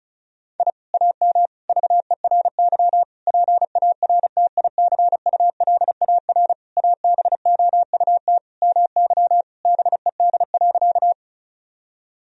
Morse code, heard as a keyed tone, about 34 wpm, 700 Hz. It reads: I AM VERY PARTICULAR ABOUT MY BED.
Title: Sentences from Top 1000 Words - Review Part 4 - 35wpm